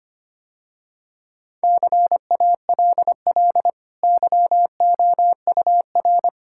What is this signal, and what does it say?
Morse code, heard as a keyed tone, about 25 wpm, 700 Hz. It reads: CALL YOUR
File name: Easy Sentences with Hints - 25wpm